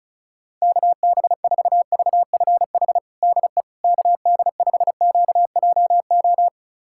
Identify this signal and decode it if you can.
Morse code, heard as a keyed tone, about 36 wpm, 700 Hz. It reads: KB4VFH DE KD5QJO
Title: USA Callsign DE Callsign 35wpm